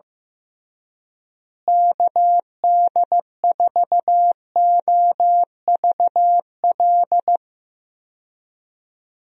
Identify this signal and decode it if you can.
Morse code, heard as a keyed tone, about 15 wpm, 700 Hz. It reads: KD4OVL
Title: USA Call Signs - 6 Characters - Repeated 3x - 15wpm